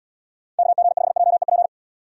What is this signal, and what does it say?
Morse code, heard as a keyed tone, about 72 wpm, 700 Hz. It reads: KG5JEP